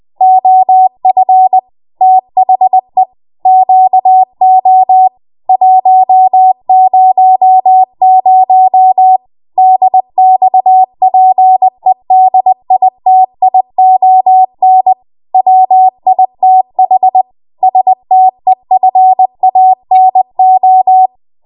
Morse code, heard as a keyed tone, about 20 wpm, 750 Hz. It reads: OF THE QO 100 DXPEDITION WITH STEFANO
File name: Morse / 20WPM